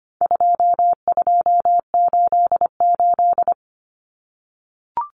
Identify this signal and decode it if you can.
Morse code, heard as a keyed tone, about 25 wpm, 700 Hz. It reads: 2288
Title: Common Cut Numbers - 4-Digit 25wpm